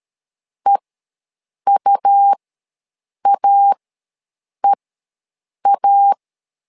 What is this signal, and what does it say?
Morse code, heard as a keyed tone, about 13 wpm, spaced as for 7 wpm, 800 Hz. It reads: EUAEA
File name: LEZIONI / 2 UHDN / 2 U ws 12wpm 1655